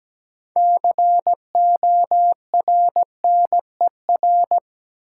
Morse code, heard as a keyed tone, about 17 wpm, 700 Hz. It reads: CORNER
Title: Six Letter Words - 17wpm